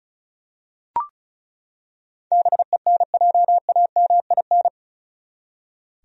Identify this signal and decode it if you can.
Morse code, heard as a keyed tone, about 36 wpm, 700 Hz. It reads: BENJAMIN
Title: USA Common Names - 35wpm